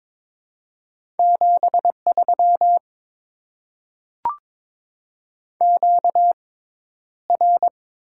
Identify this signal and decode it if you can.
Morse code, heard as a keyed tone, about 22 wpm, spaced as for 11 wpm, 690 Hz. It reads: 73 QR
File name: Top 20 QSO Elements - 6x Character Spacing - 22wpm